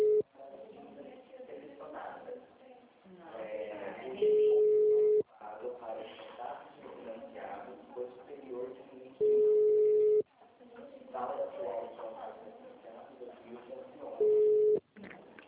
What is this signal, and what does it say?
Ringback tone (Europe, 425 Hz), with speech.
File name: out- -2007- - - .152